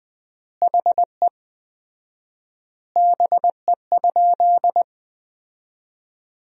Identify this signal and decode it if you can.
Morse code, heard as a keyed tone, about 20 wpm, 700 Hz. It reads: HE BE?
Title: 4x Word-Spacing - Sentences from Top 200 Words - 20wpm